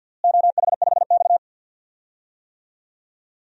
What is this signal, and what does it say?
Morse code, heard as a keyed tone, about 52 wpm, 700 Hz. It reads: OH5X
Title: DXCC-100 Call Signs - 4 Characters - Repeated 2x - 50wpm